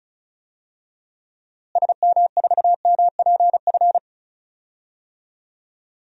Morse code, heard as a keyed tone, about 36 wpm, 700 Hz. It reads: SM4MPF
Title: DXCC-100 Call Signs - 6 Characters - Repeated 2x - 35wpm